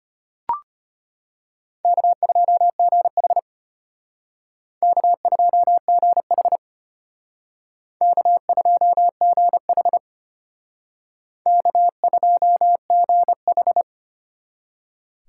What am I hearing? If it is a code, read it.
Morse code, heard as a keyed tone, about 32 wpm, 700 Hz. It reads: K2GH K2GH K2GH K2GH